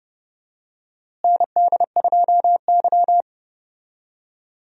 Morse code, heard as a keyed tone, about 31 wpm, 700 Hz. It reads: ND2Y
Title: USA Call Signs - 4 Characters - Repeated 2x - 30wpm